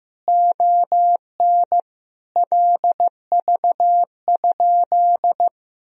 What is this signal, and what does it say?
Morse code, heard as a keyed tone, about 15 wpm, 700 Hz. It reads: ON LV?
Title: Sentences from Top 400 Words and Abbreviations 15wpm